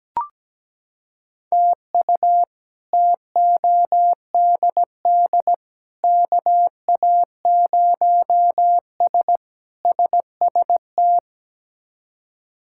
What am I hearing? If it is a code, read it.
Morse code, heard as a keyed tone, about 17 wpm, 700 Hz. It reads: TU TODD KA0S SST
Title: K1USN SST QSO Practice - 17wpm